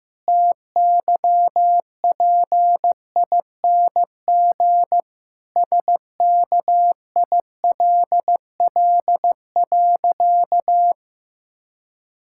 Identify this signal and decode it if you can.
Morse code, heard as a keyed tone, about 15 wpm, 700 Hz. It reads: TYPING SKILL.
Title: Pangram Challenge v2 - 15wpm